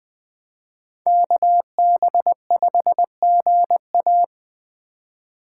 Morse code, heard as a keyed tone, about 24 wpm, 700 Hz. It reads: KB5GA
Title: Speed Racing - USA Call Signs 20wpm